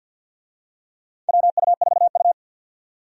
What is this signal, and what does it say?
Morse code, heard as a keyed tone, about 48 wpm, 700 Hz. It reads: WU4U
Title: USA Call Signs - 4 Characters - 50wpm